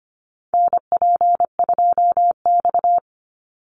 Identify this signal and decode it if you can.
Morse code, heard as a keyed tone, about 25 wpm, 700 Hz. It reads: NP2X
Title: Call Signs Repeated 2x - 25wpm